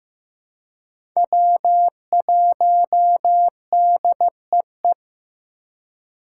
Morse code, heard as a keyed tone, about 15 wpm, 700 Hz. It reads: W1DEE
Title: USA Call Signs - 5 Characters - 3x Character Spacing - 15wpm